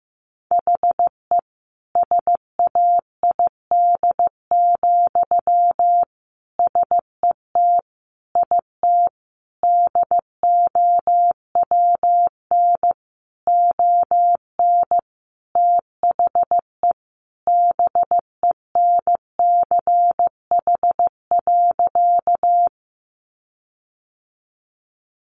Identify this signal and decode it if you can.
Morse code, heard as a keyed tone, about 15 wpm, 700 Hz. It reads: HE SAID, SET IT DOWN ON THE BENCH.
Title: Sentences from Top 1400 Words 15wpm